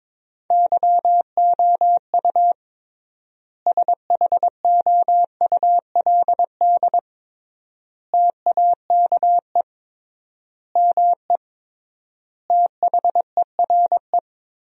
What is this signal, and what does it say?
Morse code, heard as a keyed tone, about 21 wpm, 700 Hz. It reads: YOU SHOULD TAKE ME THERE
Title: 3x Word-Spacing - Sentences from Top 300 Words - 22wpm